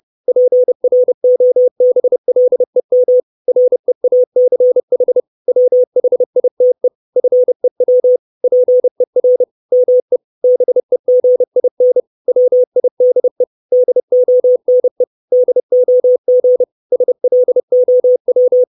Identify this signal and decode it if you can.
Morse code, heard as a keyed tone, about 30 wpm, 500 Hz. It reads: PROBLEM REACH WHITE FEW PER ME BEGIN WIDE DONE DOG SLOW
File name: 632 Words 30WPM